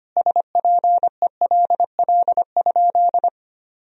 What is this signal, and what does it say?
Morse code, heard as a keyed tone, about 25 wpm, 700 Hz. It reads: SPELL?
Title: Sentences from Top 1000 Words - Review Part 1 - 25wpm